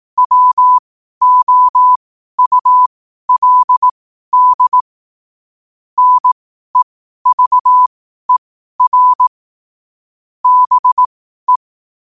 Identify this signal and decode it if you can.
Morse code, heard as a keyed tone, about 18 wpm, spaced as for 12 wpm, 1000 Hz. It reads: WOULD NEVER BE
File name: qotdfiles / 12wpm